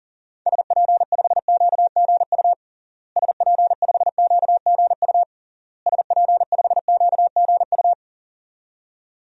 Morse code, heard as a keyed tone, about 40 wpm, 700 Hz. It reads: SP5QGU SP5QGU SP5QGU
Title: DXCC-100 Call Signs - 6 Characters - Repeated 3x - 40wpm